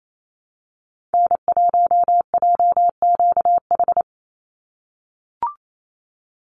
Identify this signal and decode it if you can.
Morse code, heard as a keyed tone, about 28 wpm, 700 Hz. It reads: N1JQH
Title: USA Call Signs 28wpm